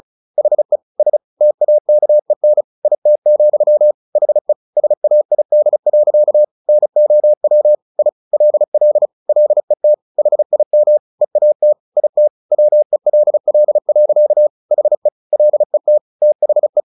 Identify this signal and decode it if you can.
Morse code, heard as a keyed tone, about 34 wpm, 600 Hz. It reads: HE S TAKEN IT, HE SAID. NOW I LL LET HIM EAT IT WELL. HE LET THE
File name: the old man and the sea / the old man and the sea 1 35wpm